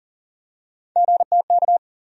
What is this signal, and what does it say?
Morse code, heard as a keyed tone, about 40 wpm, 700 Hz. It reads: GTK
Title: USA Call Sign Postfixes - 3 Letters 40wpm